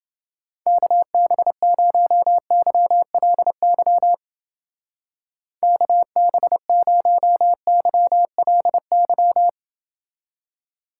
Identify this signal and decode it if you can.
Morse code, heard as a keyed tone, about 28 wpm, 700 Hz. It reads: KB0YLY KB0YLY